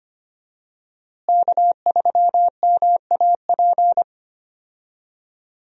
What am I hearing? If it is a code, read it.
Morse code, heard as a keyed tone, about 25 wpm, 700 Hz. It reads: K3MAP
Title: USA Call Signs - 5 Characters - Repeated 2x - 25wpm